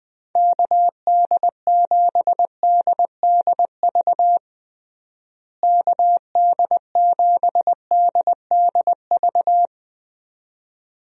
Morse code, heard as a keyed tone, about 20 wpm, 700 Hz. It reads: KD7DDV KD7DDV
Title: USA Call Signs - 6 Characters - Repeated 2x - 20wpm